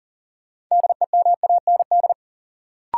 Morse code, heard as a keyed tone, about 37 wpm, 700 Hz. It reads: DEMAND
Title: Sentences from Top 800 Words - New Word Drill - 40wpm